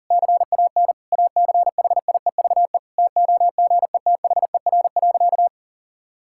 Morse code, heard as a keyed tone, about 40 wpm, 700 Hz. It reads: CAN ACHIEVE TOGETHER.
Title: Sentences from Top 1000 Words - Review Part 6 - 40wpm